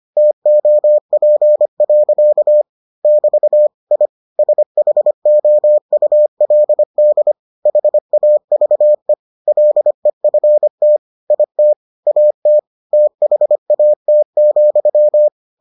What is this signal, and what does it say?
Morse code, heard as a keyed tone, about 25 wpm, 600 Hz. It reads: TOP. <BT> I SHOULD HAVE LEFT IT AT THAT,